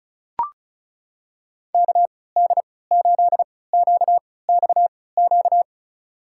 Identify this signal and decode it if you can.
Morse code, heard as a keyed tone, about 36 wpm, spaced as for 20 wpm, 700 Hz. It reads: KD8QXQ
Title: USA Call Signs - 6 Characters - 3x Character Spacing - 35wpm